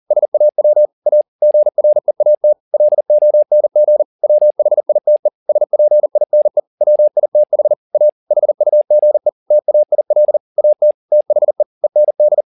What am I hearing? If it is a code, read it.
Morse code, heard as a keyed tone, about 40 wpm, 600 Hz. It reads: SAW A GREAT LONG WHITE SPINE WITH A HUGE TAIL AT THE END
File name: the old man and the sea 2 40wpm